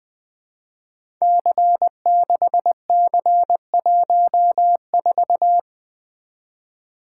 Morse code, heard as a keyed tone, about 20 wpm, 700 Hz. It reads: C6C14
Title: Code Groups - 5 Letter:Numbers - 20wpm